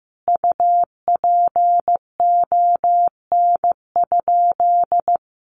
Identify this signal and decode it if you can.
Morse code, heard as a keyed tone, about 15 wpm, 700 Hz. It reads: UPON?
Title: Sentences from Top 1300 Words 15wpm